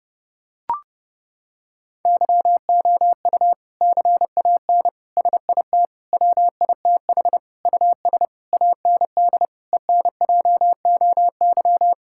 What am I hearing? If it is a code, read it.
Morse code, heard as a keyed tone, about 30 wpm, 700 Hz. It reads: YOU CAN SIT WITH US AND ENJOY
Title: Sentences from Top 1000 Words - Review Part 5 - 30wpm